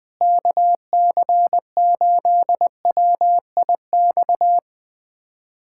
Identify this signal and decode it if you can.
Morse code, heard as a keyed tone, about 21 wpm, 700 Hz. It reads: KC8WIX
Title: USA Callsign DE Callsign 20wpm